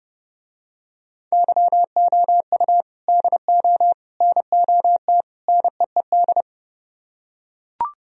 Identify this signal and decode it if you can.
Morse code, heard as a keyed tone, about 30 wpm, 700 Hz. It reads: YOU DO NOT NEED